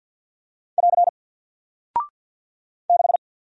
Morse code, heard as a keyed tone, about 48 wpm, 700 Hz. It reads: <AR> 6
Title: Single Character (Letter:Number:Punctuation:Prosign) 50wpm